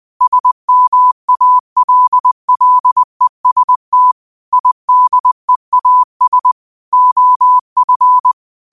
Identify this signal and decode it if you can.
Morse code, heard as a keyed tone, about 20 wpm, 1000 Hz. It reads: SMALLEST IDEAS OF